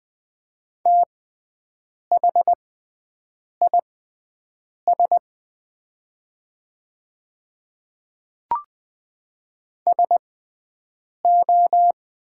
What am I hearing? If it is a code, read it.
Morse code, heard as a keyed tone, about 20 wpm, spaced as for 7 wpm, 690 Hz. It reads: THIS SO
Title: 6x Character Spacing - Top 100 Words - 20wpm